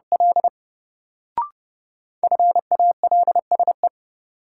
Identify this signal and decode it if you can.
Morse code, heard as a keyed tone, about 32 wpm, 700 Hz. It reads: L FALSE
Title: Sentences from Top 1600 Words - New Word Drill - 30wpm